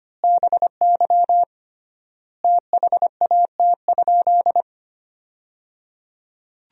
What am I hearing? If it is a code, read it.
Morse code, heard as a keyed tone, about 25 wpm, 700 Hz. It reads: BY THAT?
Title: 3x Word-Spacing - Sentences from Top 200 Words - 25wpm